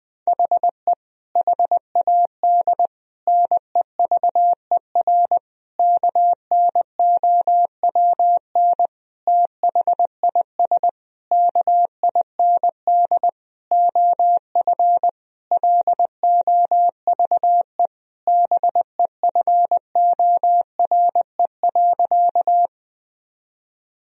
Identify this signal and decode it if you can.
Morse code, heard as a keyed tone, about 19 wpm, 700 Hz. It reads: HE HAD NEVER KNOWN THIS KIND OF LOVE BEFORE.
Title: Sentences from Top 1000 Words - Review Part 3 - 20wpm